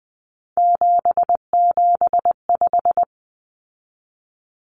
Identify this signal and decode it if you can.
Morse code, heard as a keyed tone, about 20 wpm, 700 Hz. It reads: 775
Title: Numbers - 3-Digit 20wpm